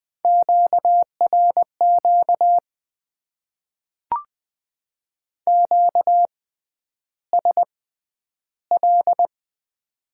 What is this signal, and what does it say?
Morse code, heard as a keyed tone, about 21 wpm, spaced as for 10 wpm, 700 Hz. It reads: QRQ QSL
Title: Top 90 QSO Elements - 6x Character Spacing - 20wpm